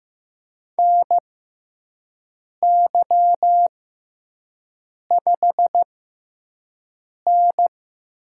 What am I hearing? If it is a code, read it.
Morse code, heard as a keyed tone, about 15 wpm, spaced as for 5 wpm, 700 Hz. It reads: NY5N
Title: USA Call Signs - 4 Characters - 6x Character Spacing - 15wpm